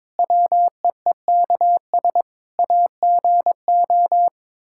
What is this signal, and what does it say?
Morse code, heard as a keyed tone, about 22 wpm, 700 Hz. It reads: WEEKS AGO
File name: practice / 22wpm / Build A Phrase - 5-Word Phrases - 22wpm